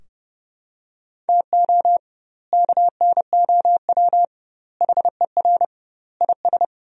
Morse code, heard as a keyed tone, about 30 wpm, 700 Hz. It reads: TO KNOW HER IS
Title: 2x Word-Spacing - Sentences from Top 400 Words - 30wpm